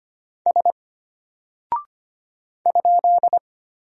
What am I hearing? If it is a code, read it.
Morse code, heard as a keyed tone, about 25 wpm, 700 Hz. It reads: S ?
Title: Single Character (Letter:Number:Punctuation:Prosign) Alt 25wpm